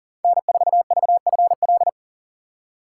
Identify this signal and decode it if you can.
Morse code, heard as a keyed tone, about 40 wpm, 700 Hz. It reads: N4VFL
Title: USA Call Signs - 5 Characters - 3x Character Spacing - 40wpm